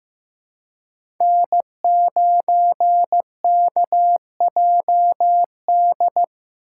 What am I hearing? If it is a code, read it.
Morse code, heard as a keyed tone, about 15 wpm, 700 Hz. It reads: N9KJD
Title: USA Call Signs - 4x Character Spacing - 15wpm